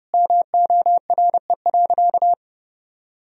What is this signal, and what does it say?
Morse code, heard as a keyed tone, about 30 wpm, 700 Hz. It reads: MORE.
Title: Sentences from Top 500 Words and Abbreviations 30wpm